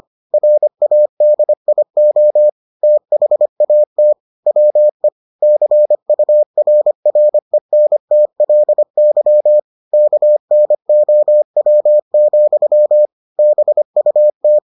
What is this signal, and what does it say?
Morse code, heard as a keyed tone, about 25 wpm, 600 Hz. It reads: RADIO THAT WE CURRENTLY KNOW, BUT